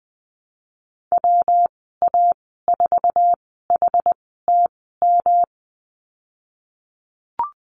Morse code, heard as a keyed tone, about 20 wpm, spaced as for 14 wpm, 700 Hz. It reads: WA4HTM
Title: 161 Lesson 26g - Letter g - Call Signs 20wpm